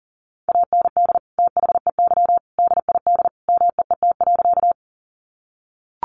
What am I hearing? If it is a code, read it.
Morse code, heard as a keyed tone, about 40 wpm, 700 Hz. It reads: AND THEY DID MEET.
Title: Sentences from Top 1100 Words 40wpm